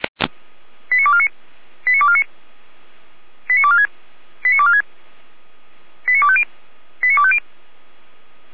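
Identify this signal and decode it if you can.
Five-tone selective call (ZVEI), heard as quick runs of five tones, tones of about 70 ms. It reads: 89249 89249 89246 89246 89250 89250
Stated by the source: ZVEI five-tone selective call received by SDR.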